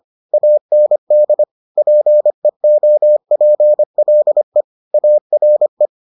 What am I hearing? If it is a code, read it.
Morse code, heard as a keyed tone, about 25 wpm, 600 Hz. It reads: AND PEOPLE ARE